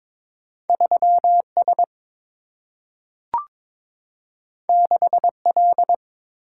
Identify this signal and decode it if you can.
Morse code, heard as a keyed tone, about 22 wpm, 690 Hz. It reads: 3S 6L
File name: practice / 22wpm / Two Letter-Number Combinations - 22wpm